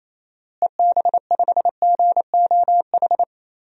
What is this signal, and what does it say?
Morse code, heard as a keyed tone, about 28 wpm, 700 Hz. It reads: EB5GOH
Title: DXCC-100 Call Signs - 6 Characters - Repeated 2x - 28wpm